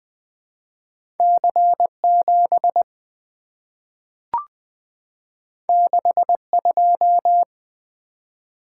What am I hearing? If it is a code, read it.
Morse code, heard as a keyed tone, about 20 wpm, 700 Hz. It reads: C7 62
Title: Two Letter-Number Combinations - 20wpm